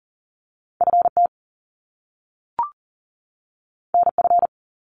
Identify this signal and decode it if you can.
Morse code, heard as a keyed tone, about 42 wpm, 700 Hz. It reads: FT NF
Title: Top Two Letter Combinations in Words 40wpm